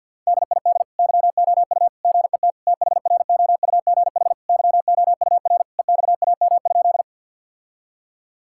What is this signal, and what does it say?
Morse code, heard as a keyed tone, about 52 wpm, 700 Hz. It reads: DID YOU GET THROUGH YOUR EXAM?
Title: Sentences from Top 1000 Words - Review Part 5 - 50wpm